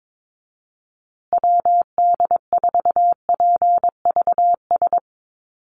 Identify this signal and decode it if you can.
Morse code, heard as a keyed tone, about 22 wpm, 700 Hz. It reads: WD4PVS
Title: USA Call Signs Repeated 2x 22wpm